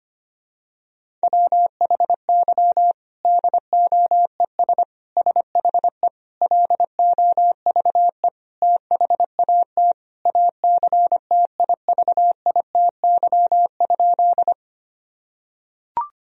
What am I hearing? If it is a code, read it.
Morse code, heard as a keyed tone, about 25 wpm, 700 Hz. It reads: WHY DOES SHE LOVE THAT ACTIVITY?
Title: Sentences from Top 1000 Words - Review Part 4 - 25wpm